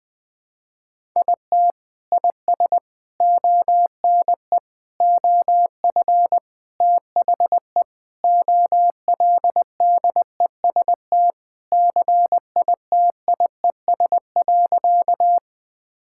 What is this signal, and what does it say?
Morse code, heard as a keyed tone, about 20 wpm, 700 Hz. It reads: IT IS ONE OF THE OLDEST CITIES.
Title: Sentences with 5-Word Phrases - 20wpm